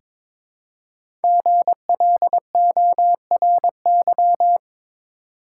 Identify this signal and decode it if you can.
Morse code, heard as a keyed tone, about 23 wpm, 700 Hz. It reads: GLORY
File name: 5 Letter Words - First 3 Letters Given - 22wpm